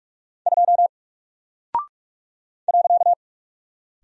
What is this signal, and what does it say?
Morse code, heard as a keyed tone, about 44 wpm, 700 Hz. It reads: 2 .
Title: Single Character (Letter:Number:Punctuation:Prosign) 45wpm